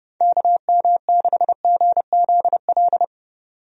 Morse code, heard as a keyed tone, about 30 wpm, 700 Hz. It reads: KM6GZL
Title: USA Call Signs - 6 Characters - 30wpm